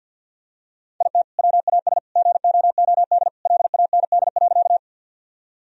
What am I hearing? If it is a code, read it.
Morse code, heard as a keyed tone, about 51 wpm, 700 Hz. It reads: IT WAS GOOD LAND.
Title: Sentences from Top 1000 Words - Review Part 1 - 50wpm